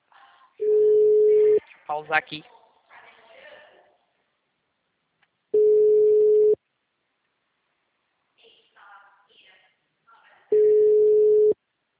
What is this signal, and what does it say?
Ringback tone (Europe, 425 Hz), with speech.
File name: out- -2036